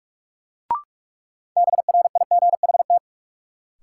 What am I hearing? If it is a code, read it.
Morse code, heard as a keyed tone, about 46 wpm, 700 Hz. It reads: BRIGHT
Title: Sentences from Top 1000 Words - New Word Drill - 45wpm